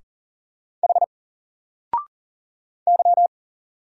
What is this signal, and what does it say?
Morse code, heard as a keyed tone, about 40 wpm, 700 Hz. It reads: H Y
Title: Single Letter-Number 40wpm